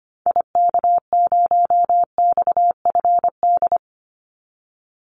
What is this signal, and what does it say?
Morse code, heard as a keyed tone, about 25 wpm, 700 Hz. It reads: IK0XFD
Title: Call Signs - 25wpm